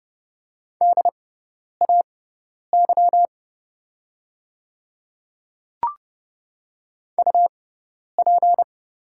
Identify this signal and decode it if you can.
Morse code, heard as a keyed tone, about 30 wpm, spaced as for 10 wpm, 700 Hz. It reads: DAY UP
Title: 6x Character Spacing - Top 100 Words - 30wpm